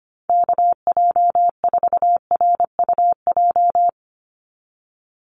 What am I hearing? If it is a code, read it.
Morse code, heard as a keyed tone, about 25 wpm, 700 Hz. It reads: KJ4RUJ
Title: USA Call Signs 25wpm